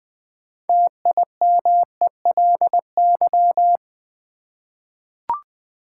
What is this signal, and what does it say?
Morse code, heard as a keyed tone, about 20 wpm, 700 Hz. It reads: TIMELY
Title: 6 Letter Words - First 3 Letters Given - 20wpm